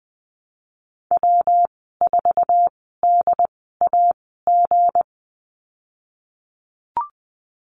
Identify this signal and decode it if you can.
Morse code, heard as a keyed tone, about 19 wpm, spaced as for 14 wpm, 700 Hz. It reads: W4DAG
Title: 193 Lesson 31g - Letter v - Call Signs 20wpm